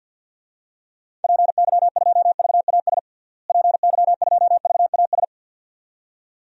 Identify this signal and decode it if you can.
Morse code, heard as a keyed tone, about 47 wpm, 700 Hz. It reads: PY2VAS PY2VAS
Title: DXCC-100 Call Signs - 6 Characters - Repeated 2x - 50wpm